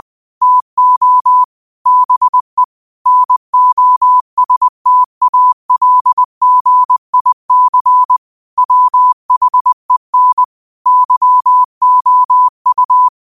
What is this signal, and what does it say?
Morse code, heard as a keyed tone, about 20 wpm, 1000 Hz. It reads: TO BE NOSTALGIC WHEN YOU
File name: qotdfiles / 20wpm